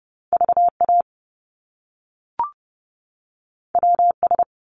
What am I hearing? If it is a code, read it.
Morse code, heard as a keyed tone, about 30 wpm, 700 Hz. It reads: VA WS
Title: Top Two Letter Combinations in Words 30wpm